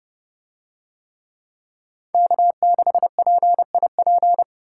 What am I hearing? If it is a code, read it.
Morse code, heard as a keyed tone, about 27 wpm, 700 Hz. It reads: K6PIP